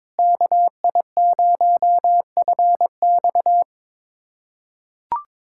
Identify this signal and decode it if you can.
Morse code, heard as a keyed tone, about 21 wpm, 700 Hz. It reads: KI0FX
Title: USA Callsign DE Callsign 22wpm